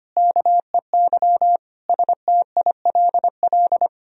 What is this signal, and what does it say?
Morse code, heard as a keyed tone, about 25 wpm, 700 Hz. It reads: KEY STILL
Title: Sentences from Top 1000 Words - Review Part 4 - 25wpm